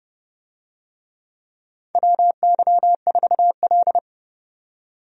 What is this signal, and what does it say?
Morse code, heard as a keyed tone, about 32 wpm, 700 Hz. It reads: WY4L
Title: USA Call Signs - 4 Characters - 30wpm